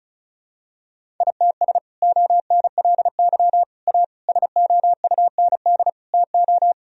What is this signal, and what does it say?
Morse code, heard as a keyed tone, about 36 wpm, 700 Hz. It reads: ITS ONLY A SOUND TO
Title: Sentences from Top 1000 Words - Review Part 1 - 35wpm